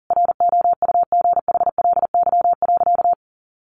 Morse code, heard as a keyed tone, about 40 wpm, 700 Hz. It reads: ROUGHLY.
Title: Sentences from Top 1500 Words 40wpm